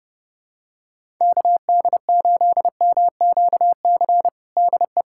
Morse code, heard as a keyed tone, about 32 wpm, 700 Hz. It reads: KD8MQC DE
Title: USA Callsign DE Callsign 30wpm